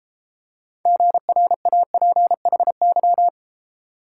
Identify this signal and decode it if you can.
Morse code, heard as a keyed tone, about 31 wpm, 700 Hz. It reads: GRAPHY